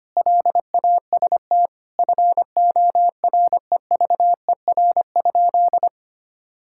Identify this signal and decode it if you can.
Morse code, heard as a keyed tone, about 25 wpm, 700 Hz. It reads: LAST FOREVER?
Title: Sentences from Top 1600 Words 25wpm